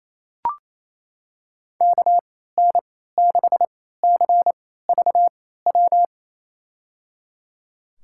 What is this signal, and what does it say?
Morse code, heard as a keyed tone, about 27 wpm, spaced as for 16 wpm, 700 Hz. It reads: KN6CVW
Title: USA Call Signs - 3x Character Spacing - 28wpm